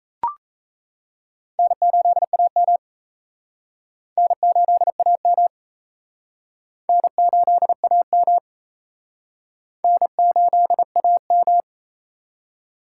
Morse code, heard as a keyed tone, about 37 wpm, 700 Hz. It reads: N8AM N8AM N8AM N8AM